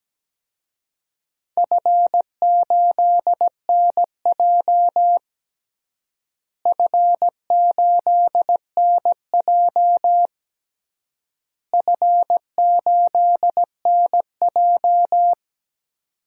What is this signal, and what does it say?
Morse code, heard as a keyed tone, about 17 wpm, 700 Hz. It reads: F8NJ F8NJ F8NJ
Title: DXCC-100 Call Signs - 4 Characters - Repeated 3x - 17wpm